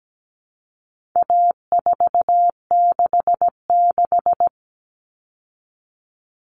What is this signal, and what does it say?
Morse code, heard as a keyed tone, about 17 wpm, 700 Hz. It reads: A466
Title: Common Cut Numbers - 4-Digit 17wpm